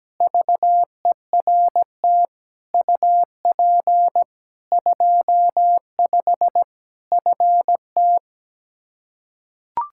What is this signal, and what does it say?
Morse code, heard as a keyed tone, about 17 wpm, 700 Hz. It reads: VERT UP 25 FT
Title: Practice QSO - Exchange 2 - Info - 17wpm